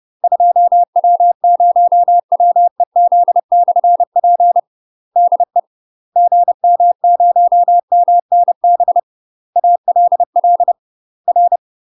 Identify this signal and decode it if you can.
Morse code, heard as a keyed tone, about 30 wpm, 700 Hz. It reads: 2W0WEZ/P DE GM0MNB ALL R
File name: UK MASTER QSOs TRAINING / 30WPM / UK MASTER QSOs 0029